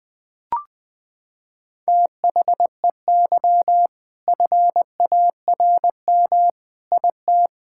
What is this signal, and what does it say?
Morse code, heard as a keyed tone, about 20 wpm, 700 Hz. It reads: THEY FARM IT
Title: Sentences from Top 1000 Words - Review Part 5 - 20wpm